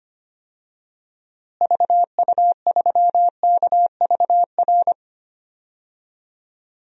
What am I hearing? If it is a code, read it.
Morse code, heard as a keyed tone, about 25 wpm, 700 Hz. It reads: VU3KVR
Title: DXCC-100 Call Signs - 6 Characters - Repeated 3x - 25wpm